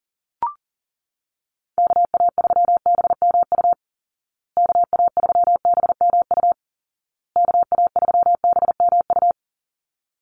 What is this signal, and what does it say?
Morse code, heard as a keyed tone, about 40 wpm, 700 Hz. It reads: KA3BMU KA3BMU KA3BMU
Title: Call Signs Repeated 2x - 40wpm